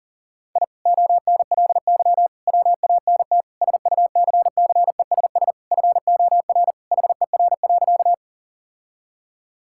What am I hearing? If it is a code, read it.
Morse code, heard as a keyed tone, about 40 wpm, 700 Hz. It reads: I ONLY WANT SUCCESS FOR HER.
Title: Sentences from Top 1000 Words - Review Part 6 - 40wpm